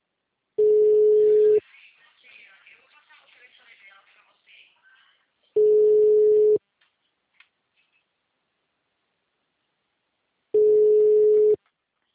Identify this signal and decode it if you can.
Ringback tone (Europe, 425 Hz), with speech.